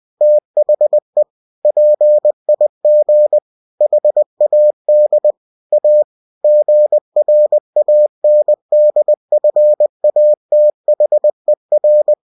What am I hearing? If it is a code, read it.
Morse code, heard as a keyed tone, about 20 wpm, 600 Hz. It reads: THE PIG HAD A GRANDFATHER
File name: cw-books / winnie-the-pooh / winnie-20wpm-page-0041